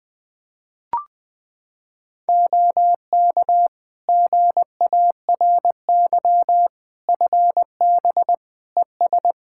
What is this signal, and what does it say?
Morse code, heard as a keyed tone, about 21 wpm, 700 Hz. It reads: OK GARY FB ES
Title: Practice QSO - Exchange 2 - Info - 20wpm